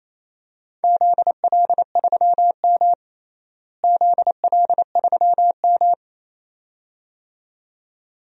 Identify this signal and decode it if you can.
Morse code, heard as a keyed tone, about 28 wpm, 700 Hz. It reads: ZL3M ZL3M
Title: DXCC-100 Call Signs - 4 Characters - Repeated 2x - 28wpm